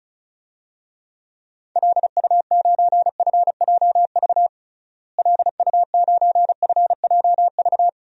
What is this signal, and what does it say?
Morse code, heard as a keyed tone, about 36 wpm, 700 Hz. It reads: LU9FJV LU9FJV
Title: DXCC-100 Call Signs - 6 Characters - Repeated 3x - 35wpm